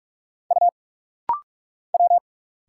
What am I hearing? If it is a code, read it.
Morse code, heard as a keyed tone, about 46 wpm, 700 Hz. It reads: U W
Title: Single Letters - Rapid-Fire 45wpm